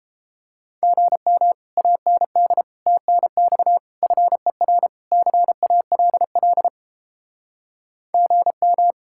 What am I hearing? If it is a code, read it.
Morse code, heard as a keyed tone, about 31 wpm, 700 Hz. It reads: GM AND TNX FER CALL GM